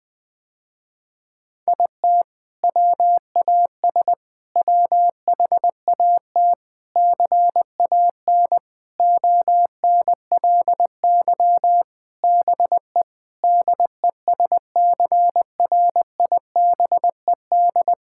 Morse code, heard as a keyed tone, about 20 wpm, 700 Hz. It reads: IT WAS WHAT CAN ONLY BE DESCRIBED
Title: Sentences with 7-Word Phrases - Spoken Beginning - 20wpm